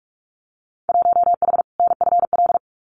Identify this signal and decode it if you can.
Morse code, heard as a keyed tone, about 46 wpm, 700 Hz. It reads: 1H NFL
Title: Winter Field Day Exchanges 45wpm